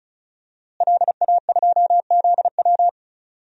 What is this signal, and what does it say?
Morse code, heard as a keyed tone, about 36 wpm, 700 Hz. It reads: LA2ZW
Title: DXCC-100 Call Signs - 5 Characters - Repeated 2x - 35wpm